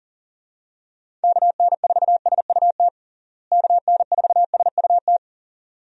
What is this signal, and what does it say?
Morse code, heard as a keyed tone, about 40 wpm, 700 Hz. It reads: KN4SUT KN4SUT
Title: USA Call Signs - 6 Characters - Repeated 2x - 40wpm